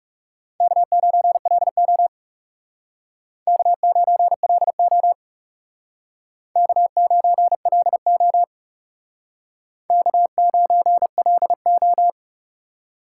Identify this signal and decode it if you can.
Morse code, heard as a keyed tone, about 40 wpm, 700 Hz. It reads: K9LO K9LO K9LO K9LO